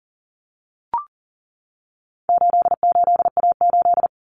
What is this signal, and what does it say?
Morse code, heard as a keyed tone, about 40 wpm, 700 Hz. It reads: 88A8
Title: Common Cut Numbers - 4-Digit 40wpm